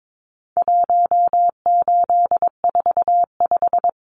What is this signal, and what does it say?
Morse code, heard as a keyed tone, about 22 wpm, 700 Hz. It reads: 1845
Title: Numbers - 4-Digit 22wpm